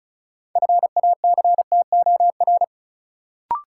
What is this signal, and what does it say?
Morse code, heard as a keyed tone, about 34 wpm, 700 Hz. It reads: FACTOR
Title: Sentences from Top 1300 Words - New Word Drill Alt - 35wpm